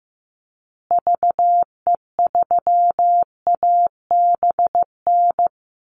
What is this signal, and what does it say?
Morse code, heard as a keyed tone, about 15 wpm, 700 Hz. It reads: VE3ABN
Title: Call Signs Repeated 2x - 15wpm